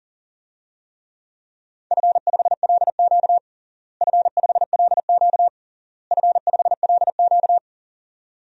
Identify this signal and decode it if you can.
Morse code, heard as a keyed tone, about 40 wpm, 700 Hz. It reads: F5LQ F5LQ F5LQ
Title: DXCC-100 Call Signs - 4 Characters - Repeated 3x - 40wpm